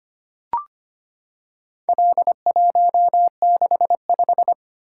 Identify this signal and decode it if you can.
Morse code, heard as a keyed tone, about 25 wpm, 700 Hz. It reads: L165
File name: practice / 25wpm / Four Letter-Number Combinations - 25wpm